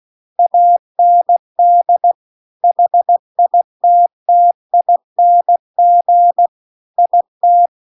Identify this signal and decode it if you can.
Morse code, heard as a keyed tone, about 16 wpm, 700 Hz. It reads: AND HITTING IT